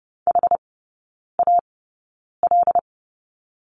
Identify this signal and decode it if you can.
Morse code, heard as a keyed tone, about 30 wpm, spaced as for 9 wpm, 700 Hz. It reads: HAL
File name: Lesson 19e - Letter w - Three Characters 30wpm